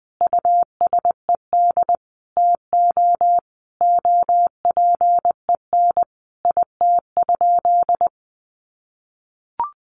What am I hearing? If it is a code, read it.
Morse code, heard as a keyed tone, about 20 wpm, 700 Hz. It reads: USED TO OPEN IT?
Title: Sentences from Top 600 Words 20wpm